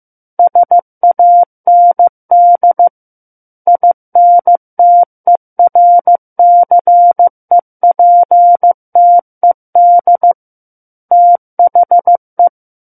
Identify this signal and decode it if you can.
Morse code, heard as a keyed tone, about 15 wpm, 700 Hz. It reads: SAND INTERCEPTED THE
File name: Book One / 06-Chapter 6 - 15 WPM